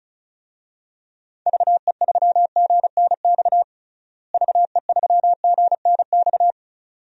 Morse code, heard as a keyed tone, about 36 wpm, 700 Hz. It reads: VE3GNX VE3GNX
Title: DXCC-100 Call Signs - 6 Characters - Repeated 2x - 35wpm